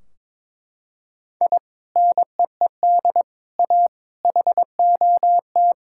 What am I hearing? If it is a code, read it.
Morse code, heard as a keyed tone, about 22 wpm, 700 Hz. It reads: I NEED A HOT
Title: Sentences from Top 1600 Words 22wpm